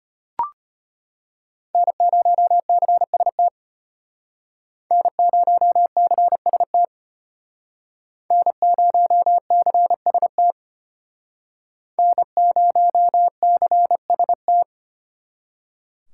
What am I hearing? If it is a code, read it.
Morse code, heard as a keyed tone, about 32 wpm, 700 Hz. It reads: N0CST N0CST N0CST N0CST